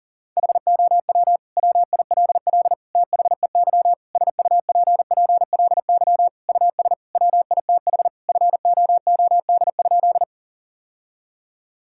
Morse code, heard as a keyed tone, about 40 wpm, 700 Hz. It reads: HOW WILL THEY SUPPLY US WITH FOOD?
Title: Sentences from Top 1000 Words - Review Part 5 - 40wpm